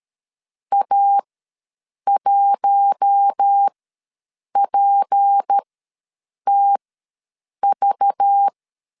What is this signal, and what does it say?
Morse code, heard as a keyed tone, about 13 wpm, spaced as for 7 wpm, 800 Hz. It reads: A1PTV
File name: LEZIONI / 8 YJ45 / 8 YJ45 ws 12wpm 3146